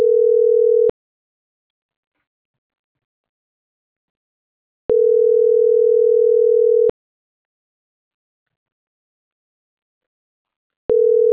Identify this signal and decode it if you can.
Ringback tone (North America), 440 + 480 Hz.